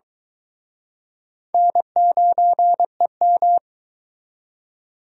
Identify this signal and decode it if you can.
Morse code, heard as a keyed tone, about 23 wpm, 700 Hz. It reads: N9EM